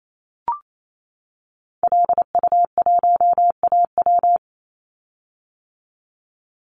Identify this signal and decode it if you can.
Morse code, heard as a keyed tone, about 28 wpm, 700 Hz. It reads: LU1AW
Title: Call Signs - 28wpm